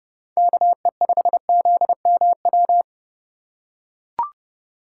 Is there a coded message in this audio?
Yes: Morse code from 0.4 to 2.8 s, heard as a keyed tone, about 29 wpm, 700 Hz. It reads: KE5ZMW